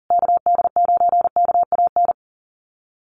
Morse code, heard as a keyed tone, about 40 wpm, 700 Hz. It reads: KD9KAN
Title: USA Call Signs Repeated 3x 40wpm